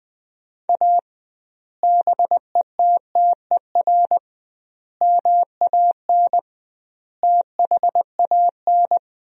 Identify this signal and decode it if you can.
Morse code, heard as a keyed tone, about 20 wpm, 700 Hz. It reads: A BETTER MAN THAN
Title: 2x Word-Spacing - Sentences from Top 400 Words - 20wpm